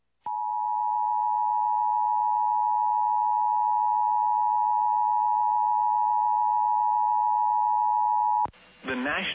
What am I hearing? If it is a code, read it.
Emergency Alert System (SAME) tones (the 853 and 960 Hz attention tone), with speech. It is heard from 0.2 to 8.5 s.
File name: EAS Logs / KMNR